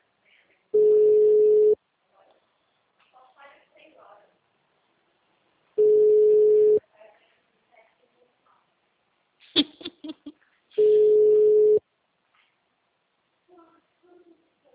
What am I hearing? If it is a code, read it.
Ringback tone (Europe, 425 Hz), with speech.